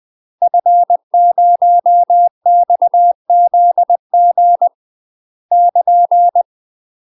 Morse code, heard as a keyed tone, about 20 wpm, 700 Hz. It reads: F0XZG <KN>
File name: UK MASTER QSOs TRAINING / 20WPM / UK MASTER QSOs 0022